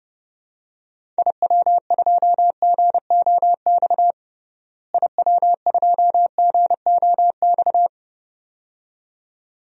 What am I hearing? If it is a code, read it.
Morse code, heard as a keyed tone, about 30 wpm, 700 Hz. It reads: IW2GOX IW2GOX
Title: DXCC-100 Call Signs - 6 Characters - Repeated 2x - 30wpm